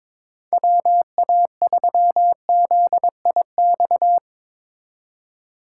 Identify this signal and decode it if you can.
Morse code, heard as a keyed tone, about 24 wpm, 700 Hz. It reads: WA3ZIX
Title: Speed Racing - USA Call Signs 22wpm